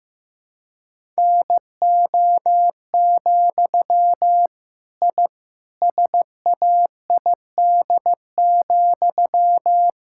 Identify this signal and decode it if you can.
Morse code, heard as a keyed tone, about 15 wpm, 700 Hz. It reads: NO, I SAID,
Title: Sentences from Top 1000 Words - Review Part 6 - 15wpm